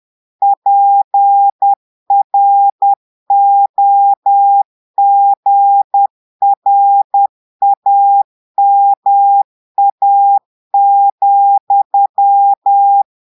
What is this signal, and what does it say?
Morse code, heard as a keyed tone, about 10 wpm, 800 Hz. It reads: PROGRAMA,
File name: portugais texte en clair 2 vitesse 10 mots minutes